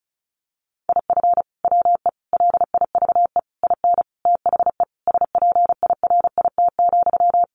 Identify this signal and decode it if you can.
Morse code, heard as a keyed tone, about 34 wpm, 700 Hz. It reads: IF WE LIVE IN THE SPIRIT,
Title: Sentences from Top 1400 Words 35wpm